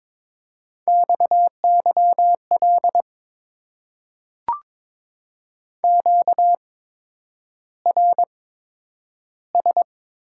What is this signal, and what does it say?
Morse code, heard as a keyed tone, about 22 wpm, spaced as for 9 wpm, 700 Hz. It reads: XYL QRS
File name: Top 90 QSO Elements - 8x Character Spacing - 22wpm